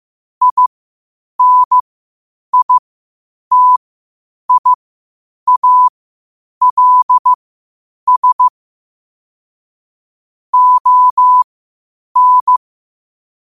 Morse code, heard as a keyed tone, about 15 wpm, spaced as for 8 wpm, 1000 Hz. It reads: INITIALS ON
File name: qotdfiles / 8wpm